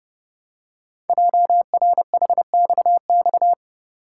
Morse code, heard as a keyed tone, about 30 wpm, 700 Hz. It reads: JRHXX
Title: Code Groups - 5 Letters - 30wpm